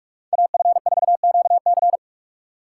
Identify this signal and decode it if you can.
Morse code, heard as a keyed tone, about 59 wpm, spaced as for 51 wpm, 690 Hz. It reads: AF4QC
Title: Speed Racing - USA 2x2 Call Signs 45wpm